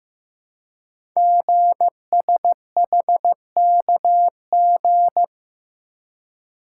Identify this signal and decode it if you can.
Morse code, heard as a keyed tone, about 15 wpm, 700 Hz. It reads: GSHKG
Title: Code Groups - 5 Letters - 15wpm